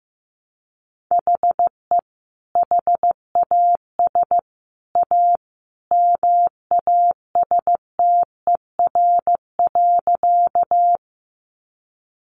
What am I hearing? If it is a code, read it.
Morse code, heard as a keyed tone, about 14 wpm, 700 Hz. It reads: HE HAS A MASTER.
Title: Sentences from Top 1300 Words 15wpm